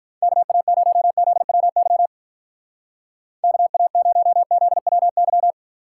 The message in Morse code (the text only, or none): KA0ZWY KA0ZWY